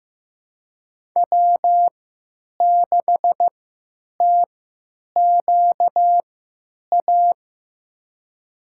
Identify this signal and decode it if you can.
Morse code, heard as a keyed tone, about 15 wpm, spaced as for 9 wpm, 700 Hz. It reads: W6TQA
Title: USA Call Signs - 5 Characters - 3x Character Spacing - 15wpm